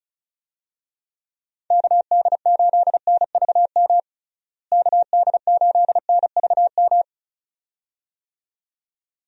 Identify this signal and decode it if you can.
Morse code, heard as a keyed tone, about 36 wpm, 700 Hz. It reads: KD8NVM KD8NVM
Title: USA Call Signs - 6 Characters - Repeated 2x - 35wpm